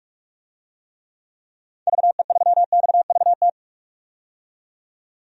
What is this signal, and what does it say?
Morse code, heard as a keyed tone, about 42 wpm, 700 Hz. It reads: VE3XVT
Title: DXCC-100 Call Signs - 6 Characters - 45wpm